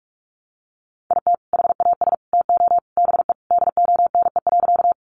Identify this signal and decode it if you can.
Morse code, heard as a keyed tone, about 44 wpm, 700 Hz. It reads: IT HAS TO BE DONE.
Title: Sentences from Top 500 Words 45wpm